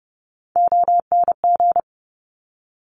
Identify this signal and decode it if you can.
Morse code, heard as a keyed tone, about 30 wpm, 690 Hz. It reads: ONG
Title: Top Three Letter Combinations in Words 30wpm